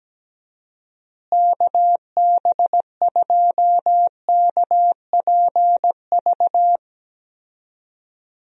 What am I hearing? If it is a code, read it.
Morse code, heard as a keyed tone, about 17 wpm, 700 Hz. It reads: KB2KPV
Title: USA Call Signs - 6 Characters - Repeated 3x - 17wpm